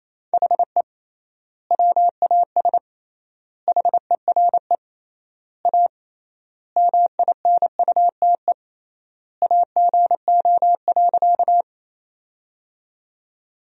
Morse code, heard as a keyed tone, about 28 wpm, 700 Hz. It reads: HE WAS HERE A MINUTE AGO.
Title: 3x Word-Spacing - Sentences from Top 500 Words - 28wpm